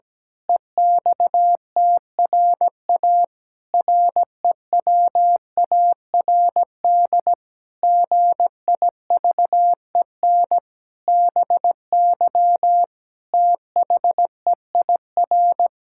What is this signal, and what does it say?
Morse code, heard as a keyed tone, about 17 wpm, 700 Hz. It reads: EXTRA REWARD GIVEN BY THEIR
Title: Pangram Challenge v2 - 17wpm